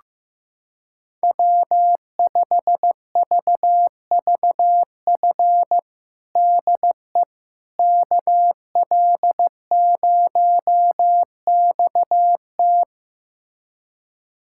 Morse code, heard as a keyed tone, about 15 wpm, 700 Hz. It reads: W5VVF DE KL0XT
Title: USA Callsign DE Callsign - 5 Characters - 15wpm